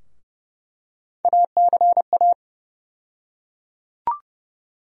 Morse code, heard as a keyed tone, about 30 wpm, 700 Hz. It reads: ACA
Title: USA Call Sign Postfixes - 3 Letters 30wpm